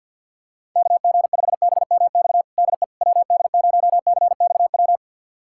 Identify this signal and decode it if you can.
Morse code, heard as a keyed tone, about 51 wpm, 700 Hz. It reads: KG5BMX DE WD0CXW
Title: USA Callsign DE Callsign - 6 Characters - 50wpm